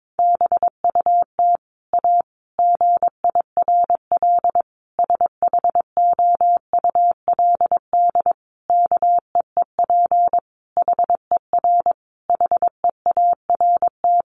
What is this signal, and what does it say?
Morse code, heard as a keyed tone, about 22 wpm, 700 Hz. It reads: BUT A GIRL SHOULD KEEP HER HEART
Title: Sentences from Top 1200 Words 22wpm